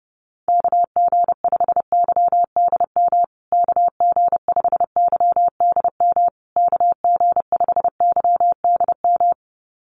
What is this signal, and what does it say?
Morse code, heard as a keyed tone, about 30 wpm, 700 Hz. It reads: KG5YDM KG5YDM KG5YDM
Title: USA Call Signs Repeated 3x 30wpm